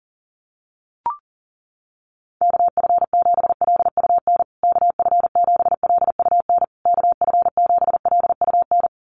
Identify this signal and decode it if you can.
Morse code, heard as a keyed tone, about 40 wpm, 700 Hz. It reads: KF7LUN KF7LUN KF7LUN
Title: USA Call Signs Repeated 3x 40wpm